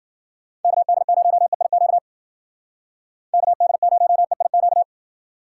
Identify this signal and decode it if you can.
Morse code, heard as a keyed tone, about 52 wpm, 700 Hz. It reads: KD0EIQ KD0EIQ